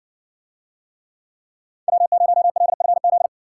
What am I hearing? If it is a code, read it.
Morse code, heard as a keyed tone, about 60 wpm, 700 Hz. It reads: W0LFZ